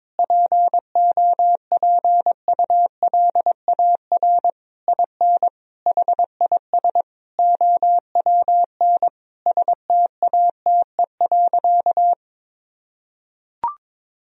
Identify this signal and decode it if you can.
Morse code, heard as a keyed tone, about 21 wpm, 700 Hz. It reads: POPULAR IN HIS OWN STATE.
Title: Sentences from Top 1000 Words - Review Part 3 - 22wpm